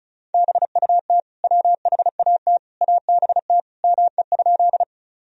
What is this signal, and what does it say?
Morse code, heard as a keyed tone, about 34 wpm, 700 Hz. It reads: BUT WHAT ABT ME?
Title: Sentences from Top 200 Words and Abbreviations 35wpm